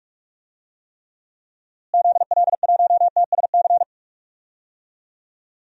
Morse code, heard as a keyed tone, about 46 wpm, 700 Hz. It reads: ZL1TSC
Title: DXCC-100 Call Signs - 6 Characters - Repeated 3x - 45wpm